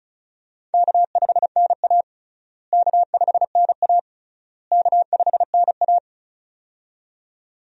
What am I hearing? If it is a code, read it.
Morse code, heard as a keyed tone, about 36 wpm, 700 Hz. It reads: K5NA K5NA K5NA
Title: USA Call Signs - 4 Characters - Repeated 3x - 35wpm